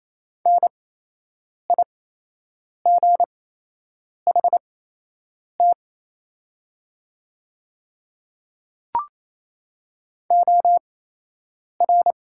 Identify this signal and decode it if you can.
Morse code, heard as a keyed tone, about 28 wpm, spaced as for 8 wpm, 700 Hz. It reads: NIGHT OR
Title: 8x Character Spacing - Top 500 Words - 28wpm